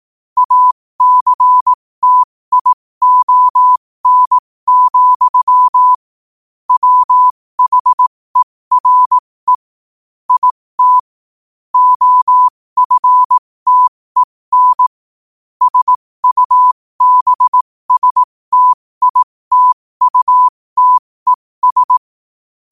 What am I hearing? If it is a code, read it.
Morse code, heard as a keyed tone, about 18 wpm, spaced as for 15 wpm, 1000 Hz. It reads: ACTION, WHERE IT OFTEN SUBSTITUTES
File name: qotdfiles / 15wpm